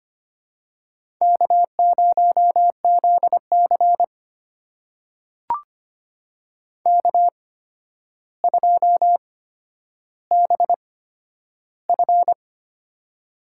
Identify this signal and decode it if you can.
Morse code, heard as a keyed tone, about 25 wpm, spaced as for 11 wpm, 700 Hz. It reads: K0ZC K2BF
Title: USA Call Signs - 4 Characters - 8x Character Spacing - 25wpm